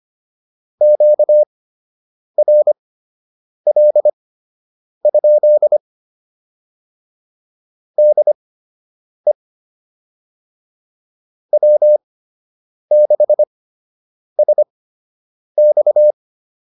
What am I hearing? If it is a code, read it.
Morse code, heard as a keyed tone, about 25 wpm, spaced as for 8 wpm, 600 Hz. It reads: QRL? DE W6SX